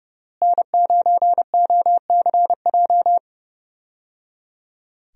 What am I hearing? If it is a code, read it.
Morse code, heard as a keyed tone, about 30 wpm, 700 Hz. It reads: N9OCJ
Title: USA Call Signs - 5 Characters - Repeated 3x - 30wpm